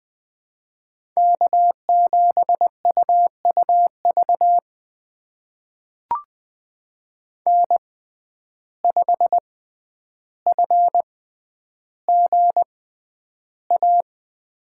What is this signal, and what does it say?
Morse code, heard as a keyed tone, about 20 wpm, spaced as for 10 wpm, 700 Hz. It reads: K7UUV N5FGA
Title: USA Call Signs - 5 Characters - 6x Character Spacing - 20wpm